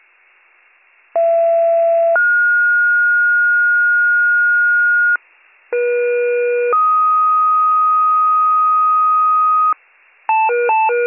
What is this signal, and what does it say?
Two-tone page, heard as a one-second tone, then a three-second tone, about 670 then 1430 Hz, about 480 then 1250 Hz, with noise. It is heard from 1.1 to 9.7 s.